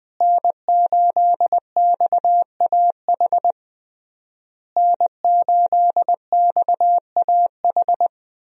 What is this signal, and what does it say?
Morse code, heard as a keyed tone, about 20 wpm, 700 Hz. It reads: N8XAH N8XAH
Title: USA Call Signs - 5 Characters - Repeated 3x - 20wpm